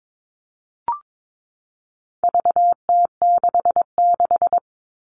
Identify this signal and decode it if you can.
Morse code, heard as a keyed tone, about 22 wpm, 700 Hz. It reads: VT66
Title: All Cut Numbers - 4-Digit 22wpm